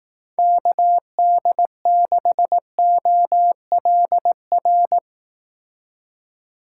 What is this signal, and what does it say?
Morse code, heard as a keyed tone, about 19 wpm, 700 Hz. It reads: KD6OLR
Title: Speed Racing - USA 2x3 Call Signs 15wpm